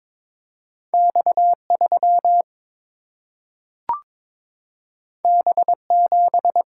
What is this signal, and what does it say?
Morse code, heard as a keyed tone, about 22 wpm, 700 Hz. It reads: X3 B7
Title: Single-Letter Single-Number 22wpm